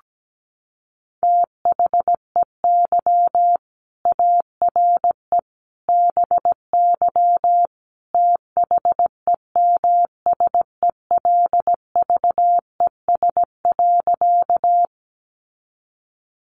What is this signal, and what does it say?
Morse code, heard as a keyed tone, about 17 wpm, 700 Hz. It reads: THEY ARE BY THEMSELVES.
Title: Sentences from Top 800 Words 17wpm